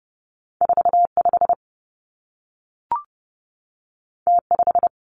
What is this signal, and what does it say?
Morse code, heard as a keyed tone, about 30 wpm, 700 Hz. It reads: 4 5 T 5
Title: Two Challenging Letter-Number Combinations 30wpm